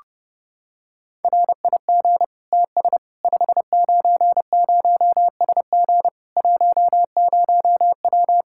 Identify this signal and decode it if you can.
Morse code, heard as a keyed tone, about 29 wpm, 700 Hz. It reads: RIG TS 590SG 10W
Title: K1USN SST QSO Practice 30wpm